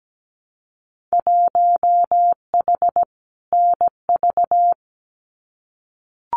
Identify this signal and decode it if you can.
Morse code, heard as a keyed tone, about 17 wpm, 700 Hz. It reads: 1H NV
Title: Winter Field Day Exchanges 17wpm